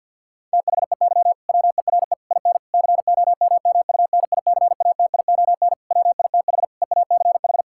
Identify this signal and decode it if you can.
Morse code, heard as a keyed tone, about 50 wpm, 700 Hz. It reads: THEY WERE IN COMMUNICATION WITH EACH
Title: Sentences from Top 1000 Words - Review Part 6 - 50wpm